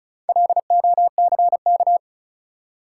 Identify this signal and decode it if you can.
Morse code, heard as a keyed tone, about 36 wpm, 700 Hz. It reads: LOCK
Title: Four Letter Words - 35wpm